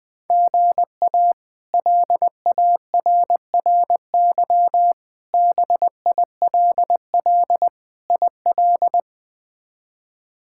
Morse code, heard as a keyed tone, about 19 wpm, 700 Hz. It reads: GA LARRY BILL IL
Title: K1USN SST QSO Practice - 20wpm